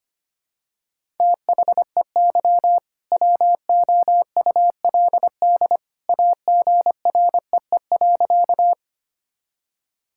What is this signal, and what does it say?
Morse code, heard as a keyed tone, about 25 wpm, 700 Hz. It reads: THEY WOULD AGREE.
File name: practice / 25wpm / Sentences with 2-Word Phrases - 25wpm